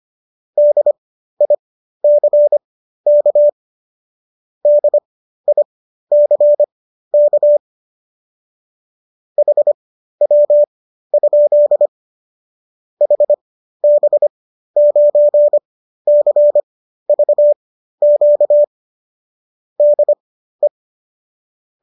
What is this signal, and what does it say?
Morse code, heard as a keyed tone, about 25 wpm, spaced as for 13 wpm, 590 Hz. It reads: DICK DICK HW? HB9CVQ DE